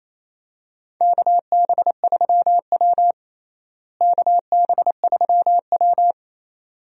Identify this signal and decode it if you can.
Morse code, heard as a keyed tone, about 28 wpm, 700 Hz. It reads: KB3W KB3W
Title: USA Call Signs - 4 Characters - Repeated 2x - 28wpm